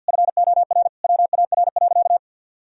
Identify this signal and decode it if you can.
Morse code, heard as a keyed tone, about 50 wpm, 700 Hz. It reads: FOR WAR.
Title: Sentences from Top 1000 Words - Review Part 3 - 50wpm